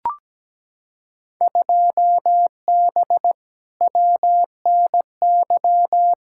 Morse code, heard as a keyed tone, about 17 wpm, 700 Hz. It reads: 2B WNY
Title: Field Day Exchanges Encore - 17wpm